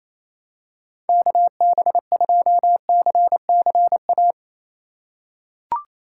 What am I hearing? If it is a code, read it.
Morse code, heard as a keyed tone, about 28 wpm, 700 Hz. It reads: KB2CCA
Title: USA Call Signs - 6 Characters - 3x Character Spacing - 28wpm